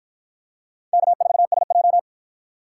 Morse code, heard as a keyed tone, about 49 wpm, 700 Hz. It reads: K4SJ